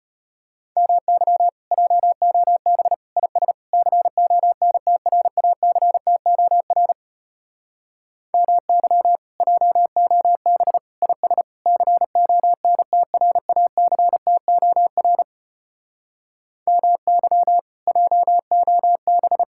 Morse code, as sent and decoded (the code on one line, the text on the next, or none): -- -.-- / .--- --- -... / .. ... / -.-. --- -. - .-. .- -.-. - --- .-. / -- -.-- / .--- --- -... / .. ... / -.-. --- -. - .-. .- -.-. - --- .-. / -- -.-- / .--- --- -...
MY JOB IS CONTRACTOR MY JOB IS CONTRACTOR MY JOB